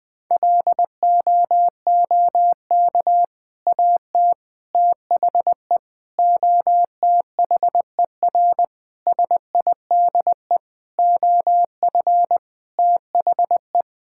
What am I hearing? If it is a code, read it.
Morse code, heard as a keyed tone, about 20 wpm, 700 Hz. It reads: LOOK AT THE OTHER SIDE OF THE
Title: Sentences with 3-Word Phrases - Spoken Beginning - 20wpm